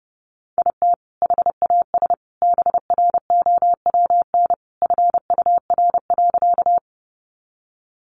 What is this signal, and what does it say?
Morse code, heard as a keyed tone, about 30 wpm, 700 Hz. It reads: IT HAS BROWN FUR.
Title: Easy Phrases 30wpm